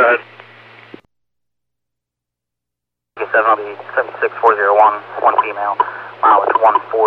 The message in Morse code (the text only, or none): none